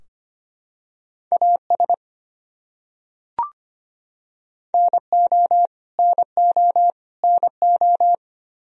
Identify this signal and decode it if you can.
Morse code, heard as a keyed tone, about 25 wpm, 700 Hz. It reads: AS NO NO NO
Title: Top 100 Words Repeated 25wpm